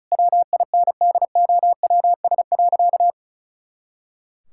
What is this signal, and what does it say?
Morse code, heard as a keyed tone, about 34 wpm, 700 Hz. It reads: WINDOWS.
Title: Pangram Challenge v2 - 35wpm